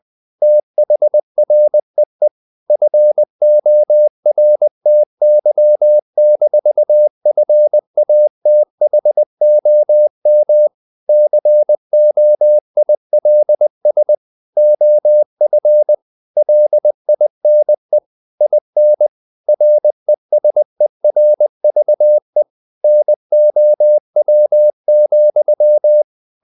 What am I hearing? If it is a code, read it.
Morse code, heard as a keyed tone, about 20 wpm, 600 Hz. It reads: THREE FORTY-FATHOM COILS OF LINE IN RESERVE NOW,
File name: the old man and the sea / the old man and the sea part 1 20wpm